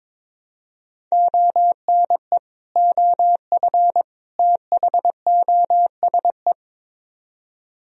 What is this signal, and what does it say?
Morse code, heard as a keyed tone, about 24 wpm, 700 Hz. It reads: ONE OF THOSE
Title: Speed-Racing - Sets of 3 Words 22wpm